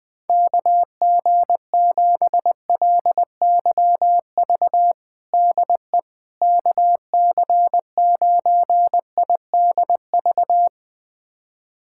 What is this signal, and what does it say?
Morse code, heard as a keyed tone, about 20 wpm, 700 Hz. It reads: KG7LYV DE KC9IDV
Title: USA Callsign DE Callsign - 6 Characters - 20wpm